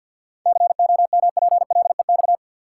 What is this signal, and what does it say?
Morse code, heard as a keyed tone, about 52 wpm, 700 Hz. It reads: COMPLEX
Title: Seven Letter Words - 50wpm